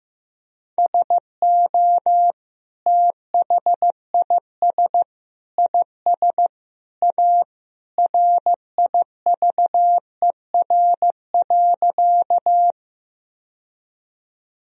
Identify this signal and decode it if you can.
Morse code, heard as a keyed tone, about 15 wpm, 700 Hz. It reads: SO THIS IS A RIVER.
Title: Sentences from Top 1000 Words - Review Part 1 - 15wpm